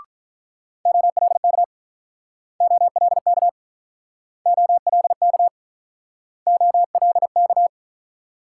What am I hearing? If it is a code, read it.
Morse code, heard as a keyed tone, about 48 wpm, 690 Hz. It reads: OLK OLK OLK OLK